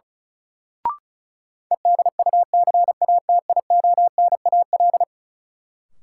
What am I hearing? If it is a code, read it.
Morse code, heard as a keyed tone, about 36 wpm, 700 Hz. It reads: EDUCATIONAL
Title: Sentences from Top 1200 Words - New Word Drill - 35wpm